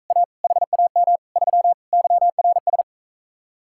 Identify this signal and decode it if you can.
Morse code, heard as a keyed tone, about 45 wpm, 700 Hz. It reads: A HAM 3 YRS